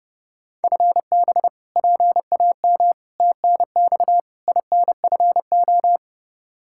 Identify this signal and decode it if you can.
Morse code, heard as a keyed tone, about 30 wpm, 700 Hz. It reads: FB PAM TNX INFO
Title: Practice QSO - Exchange 3 - More Info - 30wpm